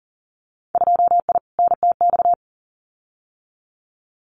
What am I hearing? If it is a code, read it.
Morse code, heard as a keyed tone, about 40 wpm, 700 Hz. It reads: 2I NTX
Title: Winter Field Day Exchanges Encore 40wpm